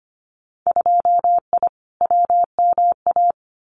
Morse code, heard as a keyed tone, about 25 wpm, 700 Hz. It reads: 2I WMA
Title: Winter Field Day Exchanges Encore 25wpm